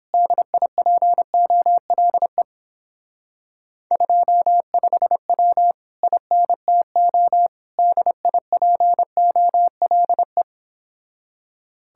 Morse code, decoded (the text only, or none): DIPOLE 25W INTO DIPOLE